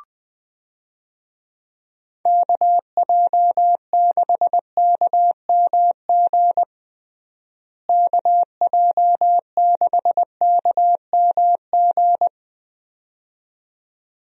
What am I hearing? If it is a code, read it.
Morse code, heard as a keyed tone, about 20 wpm, 700 Hz. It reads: KJ6KMG KJ6KMG
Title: USA Call Signs - 6 Characters - Repeated 2x - 20wpm